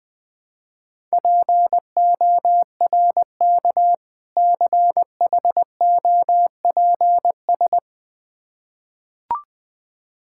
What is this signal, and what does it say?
Morse code, heard as a keyed tone, about 20 wpm, 700 Hz. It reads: PORK CHOPS
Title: Difficult Words with Hints - 20wpm